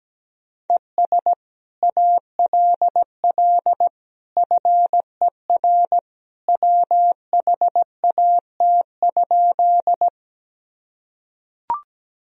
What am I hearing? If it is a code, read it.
Morse code, heard as a keyed tone, about 17 wpm, 700 Hz. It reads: ES ALL FER WHAT?
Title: Sentences from Top 100 Words and Abbreviations 17wpm